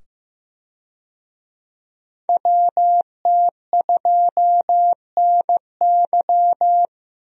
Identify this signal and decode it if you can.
Morse code, heard as a keyed tone, about 15 wpm, 700 Hz. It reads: WT2NY
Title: USA Call Signs - 5 Characters - 15wpm